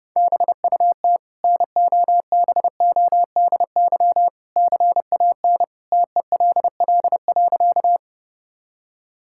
Morse code, heard as a keyed tone, about 32 wpm, 700 Hz. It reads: BUT NOBODY CAN TELL.
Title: Sentences from Top 1600 Words 30wpm